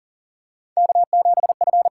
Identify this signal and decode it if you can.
Morse code, heard as a keyed tone, about 40 wpm, 700 Hz. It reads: K7F
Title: USA Call Signs - 5 Characters - 8x Character Spacing - 40wpm